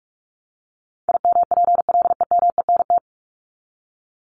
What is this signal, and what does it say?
Morse code, heard as a keyed tone, about 44 wpm, 700 Hz. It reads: IMPLEMENT
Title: Top 1501-1750 Words 45wpm